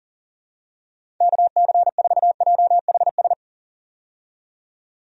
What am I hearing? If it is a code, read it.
Morse code, heard as a keyed tone, about 40 wpm, 700 Hz. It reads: KC4JHS
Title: USA Call Signs - 6 Characters - 40wpm